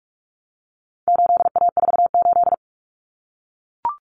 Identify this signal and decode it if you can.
Morse code, heard as a keyed tone, about 44 wpm, 700 Hz. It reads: 8A48
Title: Common Cut Numbers - 4-Digit 45wpm